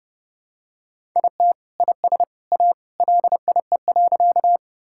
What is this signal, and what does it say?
Morse code, heard as a keyed tone, about 30 wpm, 700 Hz. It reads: IT IS A LIE.
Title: Sentences from Top 1600 Words 30wpm